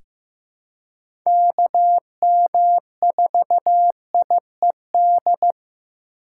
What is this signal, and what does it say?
Morse code, heard as a keyed tone, about 15 wpm, 700 Hz. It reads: KM4IED
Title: USA Call Signs - 6 Characters - 4x Character Spacing - 15wpm